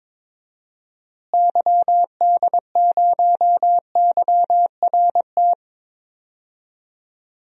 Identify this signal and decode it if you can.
Morse code, heard as a keyed tone, about 22 wpm, 700 Hz. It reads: YD0YRT
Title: DXCC-100 Call Signs - 6 Characters - Repeated 2x - 22wpm